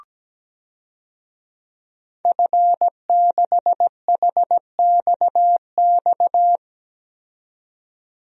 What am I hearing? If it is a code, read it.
Morse code, heard as a keyed tone, about 17 wpm, 700 Hz. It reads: F6HXX
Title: DXCC-100 Call Signs - 5 Characters - 17wpm